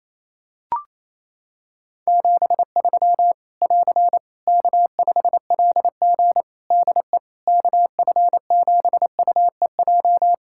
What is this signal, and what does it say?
Morse code, heard as a keyed tone, about 28 wpm, 700 Hz. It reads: 73 <AR> K5LG DE KF7UEJ
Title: K1USN SST QSO Practice 28wpm